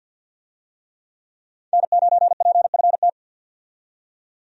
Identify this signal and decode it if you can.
Morse code, heard as a keyed tone, about 52 wpm, 700 Hz. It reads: N9PFT
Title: USA Call Signs - 5 Characters - 50wpm